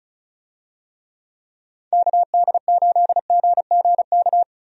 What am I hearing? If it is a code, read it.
Morse code, heard as a keyed tone, about 34 wpm, 700 Hz. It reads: KD8GGK
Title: USA Call Signs - 6 Characters - 35wpm